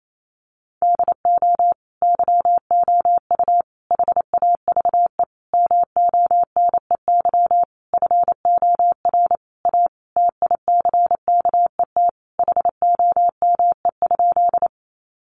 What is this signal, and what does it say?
Morse code, heard as a keyed tone, about 28 wpm, 700 Hz. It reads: DO YOU HAVE MONEY FOR A TICKET HOME?
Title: Sentences from Top 1400 Words 28wpm